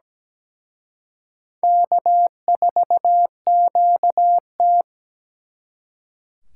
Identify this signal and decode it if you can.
Morse code, heard as a keyed tone, about 20 wpm, 700 Hz. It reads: K4QT
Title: Speed Racing - USA 1x2 Call Signs 17wpm